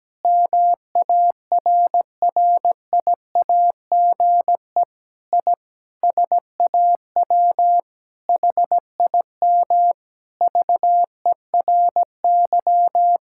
Morse code, heard as a keyed tone, about 17 wpm, 700 Hz. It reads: MARRIAGE I SAW HIM VERY
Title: Sentences from Top 1000 Words - Review Part 5 - 17wpm